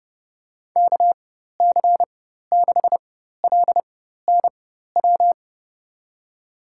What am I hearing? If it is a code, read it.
Morse code, heard as a keyed tone, about 30 wpm, spaced as for 14 wpm, 700 Hz. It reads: KC6LNW
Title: USA Call Signs - 6 Characters - 4x Character Spacing - 30wpm